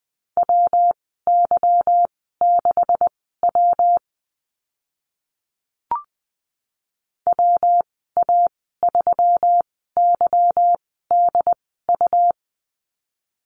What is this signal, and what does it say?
Morse code, heard as a keyed tone, about 20 wpm, spaced as for 14 wpm, 700 Hz. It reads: WY6W WA3YDU
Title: 167 Lesson 27g - Number 7 - Call Signs 20wpm